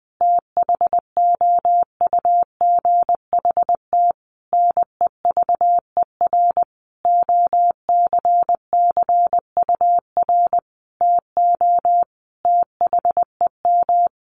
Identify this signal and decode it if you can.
Morse code, heard as a keyed tone, about 20 wpm, 700 Hz. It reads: THOUGHT NEVER OCCUR TO THEM
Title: Sentences from Top 900 Words 20wpm